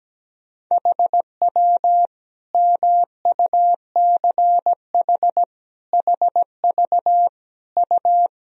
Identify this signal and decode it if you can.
Morse code, heard as a keyed tone, about 17 wpm, 700 Hz. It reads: HW MUCH HV U
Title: Sentences from Top 300 Words and Abbreviations 17wpm